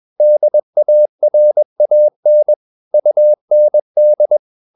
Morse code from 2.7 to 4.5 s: UND